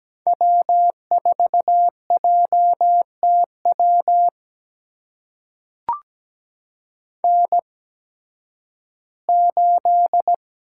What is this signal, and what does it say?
Morse code, heard as a keyed tone, about 17 wpm, spaced as for 11 wpm, 700 Hz. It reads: W4JTW N8
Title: USA Call Signs - 5 Characters - 8x Character Spacing - 17wpm